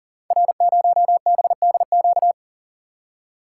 Morse code, heard as a keyed tone, about 38 wpm, 690 Hz. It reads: R0BDQ
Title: Code Groups - 5 Letter:Numbers - 40wpm